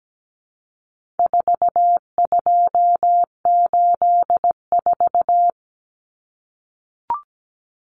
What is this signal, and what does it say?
Morse code, heard as a keyed tone, about 17 wpm, 700 Hz. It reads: 4284
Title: Common Cut Numbers - 4-Digit 17wpm